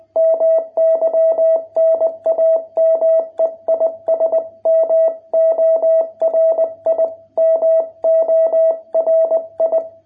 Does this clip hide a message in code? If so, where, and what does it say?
Yes: Morse code from 0.2 to 9.8 s, heard as a keyed tone, about 20 wpm, 660 Hz. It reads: MY NAME IS MORI MORI